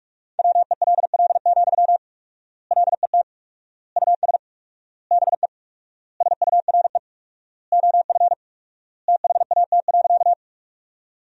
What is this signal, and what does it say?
Morse code, heard as a keyed tone, about 46 wpm, 700 Hz. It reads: WELL, LET US BE SURE OF THAT.
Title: 4x Word-Spacing - Sentences from Top 300 Words - 45wpm